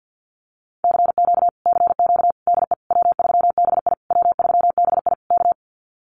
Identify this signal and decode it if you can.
Morse code, heard as a keyed tone, about 48 wpm, 700 Hz. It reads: CQ CQ DE W3BI W3BI K
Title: Common QSO Phrases Encore 50wpm